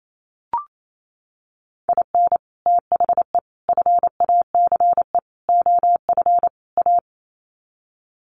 Morse code, heard as a keyed tone, about 28 wpm, 700 Hz. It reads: IN THE FACE OF A
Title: Sets of 5 Words Encore 28wpm